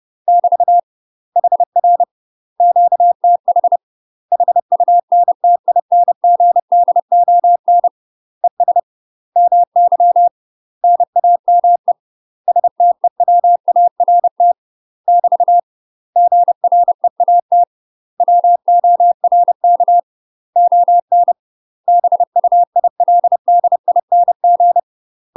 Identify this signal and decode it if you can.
Morse code, heard as a keyed tone, about 30 wpm, 700 Hz. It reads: <BT> HR QTH HUNTINGDON ES MY NAME STEWART <BT> GREAT WORK ON BUILDING
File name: UK MASTER QSOs TRAINING / 30WPM / UK MASTER QSOs 0038